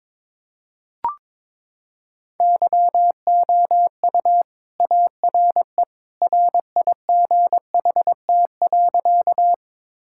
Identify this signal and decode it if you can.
Morse code, heard as a keyed tone, about 21 wpm, 700 Hz. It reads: YOU ARE RIGHT.
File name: practice / 22wpm / Sentences with 2-Word Phrases - 22wpm